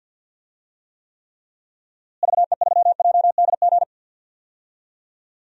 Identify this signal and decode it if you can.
Morse code, heard as a keyed tone, about 49 wpm, 700 Hz. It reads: VE3JDG
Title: DXCC-100 Call Signs - 6 Characters - Repeated 3x - 50wpm